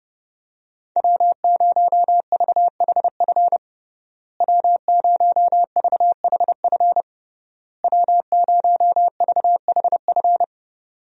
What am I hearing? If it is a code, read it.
Morse code, heard as a keyed tone, about 30 wpm, 700 Hz. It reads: W0VHF W0VHF W0VHF
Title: USA Call Signs - 5 Characters - Repeated 3x - 30wpm